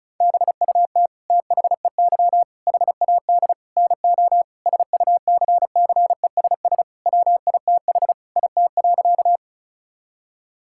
Morse code, heard as a keyed tone, about 36 wpm, 700 Hz. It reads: BUT THEY HAD NO SUCCESS WITH IT.
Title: Sentences from Top 1000 Words - Review Part 6 - 35wpm